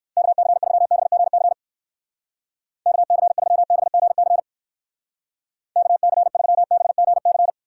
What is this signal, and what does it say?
Morse code, heard as a keyed tone, about 58 wpm, 700 Hz. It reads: KC3BGC KC3BGC KC3BGC